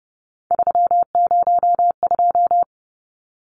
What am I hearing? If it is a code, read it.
Morse code, heard as a keyed tone, about 30 wpm, 700 Hz. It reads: 302
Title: Numbers - 3-Digit 30wpm